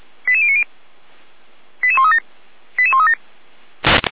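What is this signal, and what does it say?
Five-tone selective call (ZVEI), heard as quick runs of five tones, tones of about 70 ms. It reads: 88090 88137 88137
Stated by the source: ZVEI five-tone selective call received on an SDR, Feuerwehr Orlamünde.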